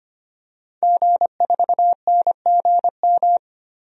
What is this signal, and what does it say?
Morse code, heard as a keyed tone, about 25 wpm, 700 Hz. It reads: G4NGM
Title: DXCC-100 Call Signs - 5 Characters - 25wpm